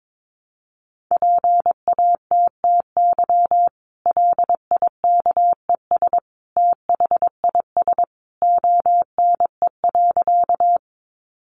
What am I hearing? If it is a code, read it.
Morse code, heard as a keyed tone, about 22 wpm, 700 Hz. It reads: PATTY LIKES THIS ONE.
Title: Additional Easy Phrases 22wpm